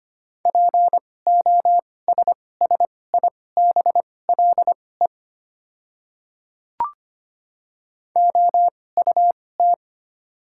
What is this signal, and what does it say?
Morse code, heard as a keyed tone, about 25 wpm, spaced as for 18 wpm, 700 Hz. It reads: POSSIBLE OUT
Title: 2x Character Spacing - Top 500 Words - 25wpm